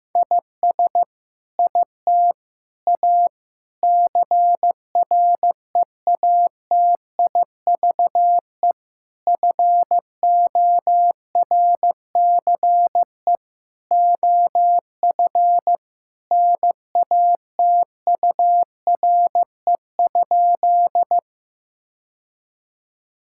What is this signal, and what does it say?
Morse code, heard as a keyed tone, about 16 wpm, 700 Hz. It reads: IS IT A CREATIVE FORCE OF NATURE?
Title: Sentences from Top 1000 Words - Review Part 5 - 15wpm